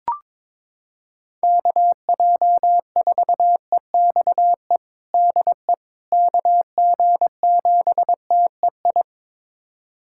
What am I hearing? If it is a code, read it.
Morse code, heard as a keyed tone, about 22 wpm, 700 Hz. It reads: KJ4EXE DE KG7TEI
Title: USA Callsign DE Callsign - 6 Characters - 22wpm